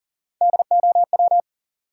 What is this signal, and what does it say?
Morse code, heard as a keyed tone, about 38 wpm, 700 Hz. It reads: DOW
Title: Sets of 5 Words - New Word Drill - 40wpm